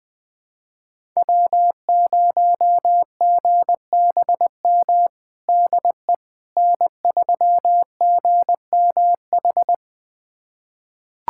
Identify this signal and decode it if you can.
Morse code, heard as a keyed tone, about 20 wpm, 700 Hz. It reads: W0GBM DE N3GMH
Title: USA Callsign DE Callsign - 5 Characters - 20wpm